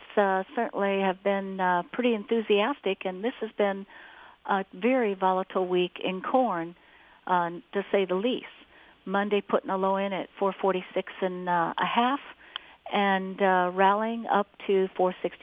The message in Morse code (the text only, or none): none